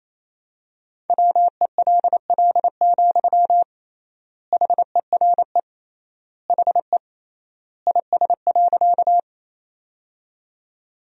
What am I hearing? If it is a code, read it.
Morse code, heard as a keyed tone, about 27 wpm, 700 Hz. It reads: WELL, HERE HE IS.
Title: 3x Word-Spacing - Sentences from Top 200 Words - 28wpm